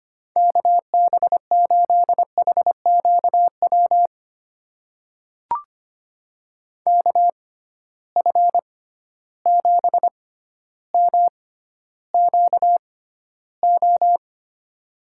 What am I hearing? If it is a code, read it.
Morse code, heard as a keyed tone, about 25 wpm, spaced as for 13 wpm, 700 Hz. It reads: KB8HQW KF7MQO
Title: USA Call Signs - 6 Characters - 6x Character Spacing - 25wpm